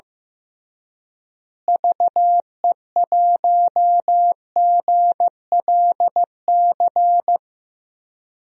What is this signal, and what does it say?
Morse code, heard as a keyed tone, about 16 wpm, 700 Hz. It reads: VE1GLC
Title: DXCC-100 Call Signs - 6 Characters - Repeated 2x - 15wpm